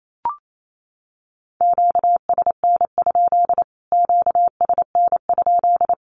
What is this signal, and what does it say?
Morse code, heard as a keyed tone, about 28 wpm, 700 Hz. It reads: QSN? QSN?
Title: Q-Codes Repeated 3x 28wpm